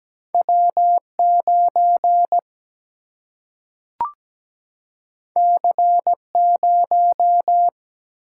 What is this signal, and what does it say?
Morse code, heard as a keyed tone, about 17 wpm, 700 Hz. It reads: W 9 C 0
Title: Single-Letter Single-Number 17wpm